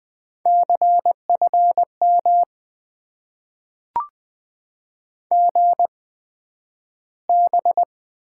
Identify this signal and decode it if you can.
Morse code, heard as a keyed tone, about 20 wpm, spaced as for 11 wpm, 700 Hz. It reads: CFM GB
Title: Top 90 QSO Elements - 8x Character Spacing - 20wpm